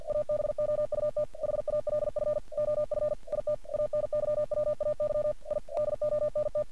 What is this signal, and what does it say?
Morse code, heard as a keyed tone, about 50 wpm, 600 Hz. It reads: ABOUT HALF OF IT ANYWAY I DONT
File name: archived / CRSnet